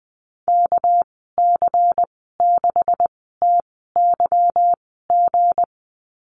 Morse code, heard as a keyed tone, about 20 wpm, spaced as for 15 wpm, 700 Hz. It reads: KC6TYG
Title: 200 Lesson 32g - Letter k - Call Signs 20wpm